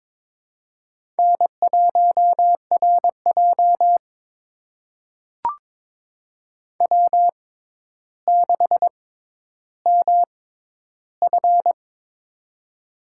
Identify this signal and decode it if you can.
Morse code, heard as a keyed tone, about 22 wpm, spaced as for 11 wpm, 700 Hz. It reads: N1RJ W6MF
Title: USA Call Signs - 4 Characters - 6x Character Spacing - 22wpm